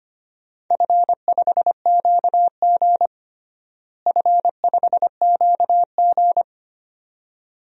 Morse code, heard as a keyed tone, about 25 wpm, 700 Hz. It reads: F5QG F5QG
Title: DXCC-100 Call Signs - 4 Characters - Repeated 2x - 25wpm